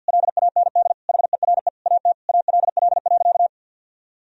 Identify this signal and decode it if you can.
Morse code, heard as a keyed tone, about 50 wpm, 700 Hz. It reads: LAND HERE AT ALL.
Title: Sentences from Top 1000 Words - Review Part 1 - 50wpm